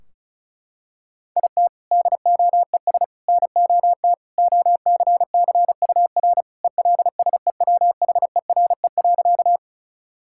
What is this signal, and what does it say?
Morse code, heard as a keyed tone, about 34 wpm, 700 Hz. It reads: IT DOES NOT OCCUR ELSEWHERE.
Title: Sentences from Top 1600 Words 35wpm